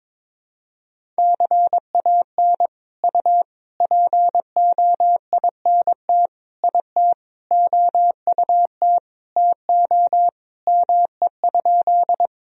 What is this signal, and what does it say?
Morse code, heard as a keyed tone, about 22 wpm, 700 Hz. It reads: CAN U POINT IT OUT TO ME?
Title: Sentences from Top 200 Words and Abbreviations 22wpm